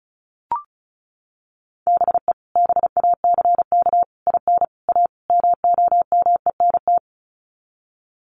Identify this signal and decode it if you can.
Morse code, heard as a keyed tone, about 34 wpm, 700 Hz. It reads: BE BACK IN A MOMENT
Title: Sets of 5 Words 35wpm